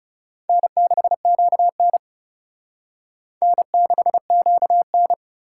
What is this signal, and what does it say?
Morse code, heard as a keyed tone, about 38 wpm, 700 Hz. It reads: N6QN N6QN